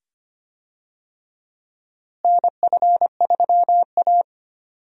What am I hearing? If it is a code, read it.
Morse code, heard as a keyed tone, about 25 wpm, 700 Hz. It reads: NF3A
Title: USA Call Signs - 4 Characters - Repeated 2x - 25wpm